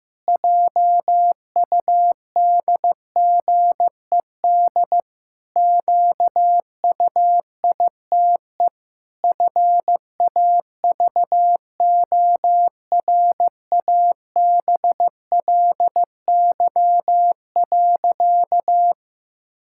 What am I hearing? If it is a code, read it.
Morse code, heard as a keyed tone, about 15 wpm, 700 Hz. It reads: JUDGED QUITE FAVORABLY.
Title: Pangram Challenge v2 - 15wpm